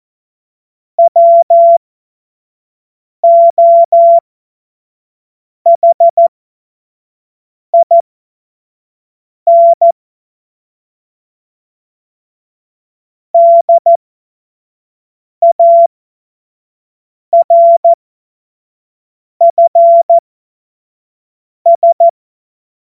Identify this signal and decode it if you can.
Morse code, heard as a keyed tone, about 14 wpm, spaced as for 5 wpm, 680 Hz. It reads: WOHIN DARFS
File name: Morsekurs / Witze / Witz 16 Januar 2024 25 F70